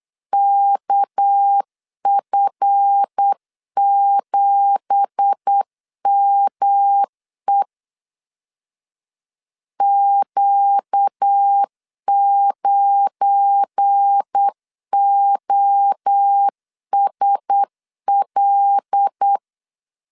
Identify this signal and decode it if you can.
Morse code, heard as a keyed tone, about 8 wpm, 800 Hz. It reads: KF7ME Q9OSL
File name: CW / DET / 1 9 1500C S B 8wpm 5624